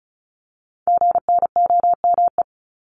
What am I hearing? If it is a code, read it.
Morse code, heard as a keyed tone, about 36 wpm, 700 Hz. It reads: GNOME
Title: Multi-letter Phonemes 35wpm